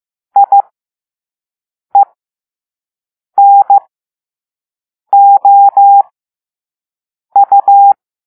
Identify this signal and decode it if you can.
Morse code, heard as a keyed tone, about 15 wpm, spaced as for 5 wpm, 800 Hz. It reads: IENOU
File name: Morse / Level5-All-B